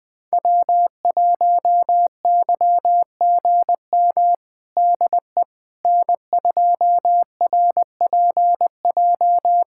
Morse code, heard as a keyed tone, about 20 wpm, 700 Hz. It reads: W1YGM DE N2RPJ
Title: USA Callsign DE Callsign - 5 Characters - 20wpm